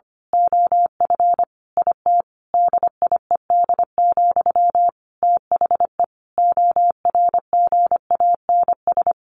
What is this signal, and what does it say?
Morse code, heard as a keyed tone, about 25 wpm, 700 Hz. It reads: OF IT DIED, THE ORGANS
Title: A Sound Of Thunder 25wpm